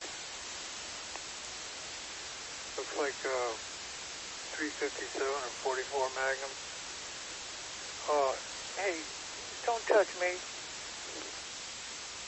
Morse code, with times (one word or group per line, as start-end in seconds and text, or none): none